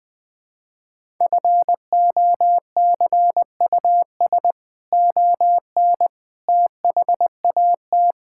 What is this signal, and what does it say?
Morse code, heard as a keyed tone, about 20 wpm, 700 Hz. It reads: FOCUS ON THAT
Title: Sentences from Top 1000 Words - Review Part 5 - 20wpm